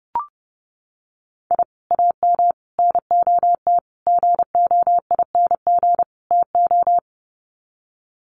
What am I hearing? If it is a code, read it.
Morse code, heard as a keyed tone, about 30 wpm, 700 Hz. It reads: I AM NOT GOING TO
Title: Sets of 5 Words Encore 30wpm